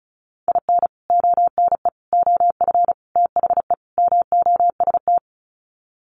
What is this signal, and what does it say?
Morse code, heard as a keyed tone, about 34 wpm, 700 Hz. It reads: IN ONE OF THE MOST
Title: Sets of 5 Words Encore 35wpm